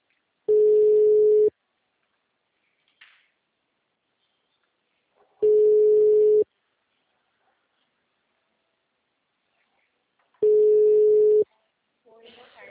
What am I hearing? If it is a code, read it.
Ringback tone (Europe, 425 Hz), with speech.
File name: out- -2003- - - .27702